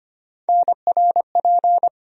Morse code, heard as a keyed tone, about 25 wpm, 700 Hz. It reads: NRP
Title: USA Call Sign Postfixes - 3 Letters 25wpm